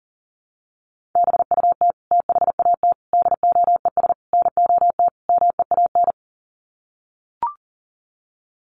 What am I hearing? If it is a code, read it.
Morse code, heard as a keyed tone, about 40 wpm, 700 Hz. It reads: BUT THAT DOES NOT MEAN
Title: Sets of 5 Words 40wpm